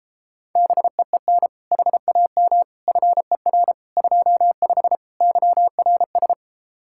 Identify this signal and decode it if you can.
Morse code, heard as a keyed tone, about 33 wpm, 700 Hz. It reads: BEEN HAM FER 25 YRS